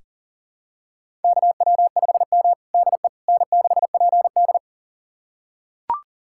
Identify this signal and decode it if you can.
Morse code, heard as a keyed tone, about 40 wpm, 700 Hz. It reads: KW5M DE N6PD
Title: USA Callsign DE Callsign - 4 Characters - 40wpm